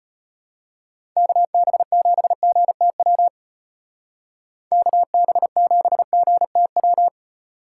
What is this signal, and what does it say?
Morse code, heard as a keyed tone, about 34 wpm, 700 Hz. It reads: KB7GTW KB7GTW